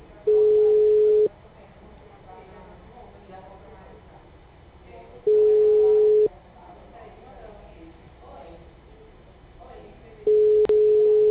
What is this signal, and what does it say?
Ringback tone (Europe, 425 Hz), with speech.